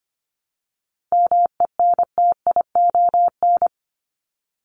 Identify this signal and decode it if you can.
Morse code, heard as a keyed tone, about 25 wpm, 700 Hz. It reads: MENTION
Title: ED Suffix - Top 100 Words with Base Words 25wpm